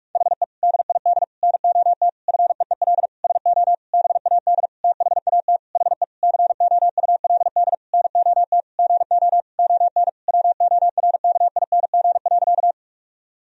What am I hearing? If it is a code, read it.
Morse code, heard as a keyed tone, about 46 wpm, 700 Hz. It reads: HE DID NOT FEEL SO BAD THAT HE COULD NOT GO ON WORKING.
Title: Sentences from Top 1000 Words - Review Part 4 - 45wpm